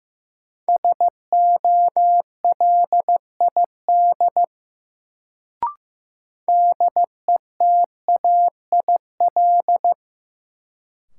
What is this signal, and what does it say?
Morse code, heard as a keyed tone, about 15 wpm, 700 Hz. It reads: SOLID DETAIL
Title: Sentences from Top 1500 Words - New Word Drill - 15wpm